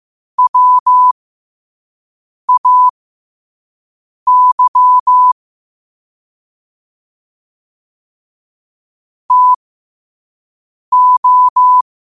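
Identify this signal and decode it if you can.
Morse code, heard as a keyed tone, about 15 wpm, spaced as for 5 wpm, 1000 Hz. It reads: WAY TO
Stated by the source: Morse code sent at 5 words per minute.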